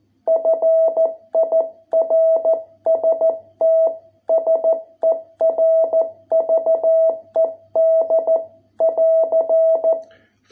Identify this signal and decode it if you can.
Morse code, heard as a keyed tone, about 13 wpm, 660 Hz. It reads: FIRST SERVED <AR>